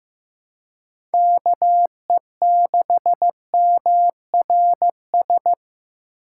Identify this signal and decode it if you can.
Morse code, heard as a keyed tone, about 15 wpm, 700 Hz. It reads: KE6MRS
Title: USA Call Signs - 4x Character Spacing - 15wpm